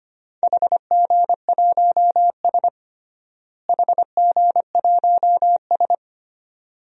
Morse code, heard as a keyed tone, about 25 wpm, 700 Hz. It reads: HG1S HG1S
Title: DXCC-100 Call Signs - 4 Characters - Repeated 2x - 25wpm